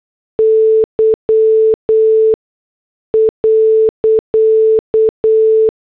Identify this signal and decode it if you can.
Morse code, heard as a keyed tone, about 8 wpm, spaced as for 6 wpm, 440 Hz. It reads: Y.
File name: morse / headlines